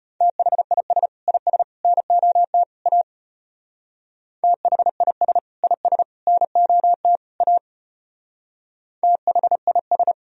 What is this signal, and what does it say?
Morse code, heard as a keyed tone, about 31 wpm, 700 Hz. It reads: THIS IS NOT A THIS IS NOT A THIS
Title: Speed-Racing - Sets of 4 Words 25wpm